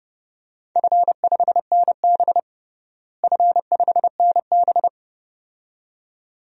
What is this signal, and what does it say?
Morse code, heard as a keyed tone, about 32 wpm, 700 Hz. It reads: F5NB F5NB
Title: DXCC-100 Call Signs - 4 Characters - Repeated 2x - 30wpm